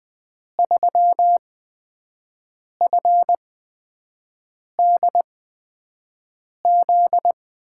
Morse code, heard as a keyed tone, about 20 wpm, spaced as for 5 wpm, 700 Hz. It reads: 3FDZ
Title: USA Call Signs - 6 Characters - 8x Character Spacing - 20wpm